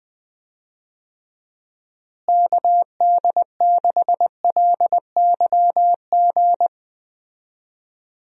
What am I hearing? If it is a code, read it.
Morse code, heard as a keyed tone, about 20 wpm, 700 Hz. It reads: KD6LYG
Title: USA Call Signs - 6 Characters - Repeated 2x - 20wpm